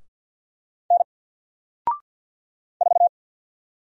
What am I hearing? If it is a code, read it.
Morse code, heard as a keyed tone, about 48 wpm, 700 Hz. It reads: N 4
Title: Single Letter-Number Alt 50wpm